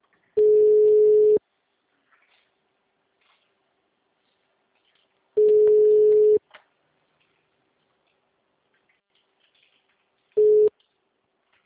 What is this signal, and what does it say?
Ringback tone (Europe, 425 Hz).
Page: Sumare - SP